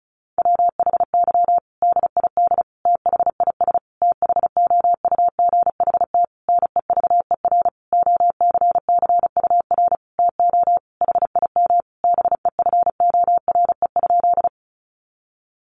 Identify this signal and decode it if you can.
Morse code, heard as a keyed tone, about 34 wpm, 700 Hz. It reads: WHY DID THIS THOUGHT NEVER OCCUR TO HIM BEFORE?
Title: Sentences from Top 900 Words 35wpm